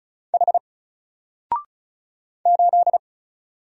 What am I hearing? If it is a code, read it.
Morse code, heard as a keyed tone, about 34 wpm, 700 Hz. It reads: H 8
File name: Single-Letter-Number-35wpm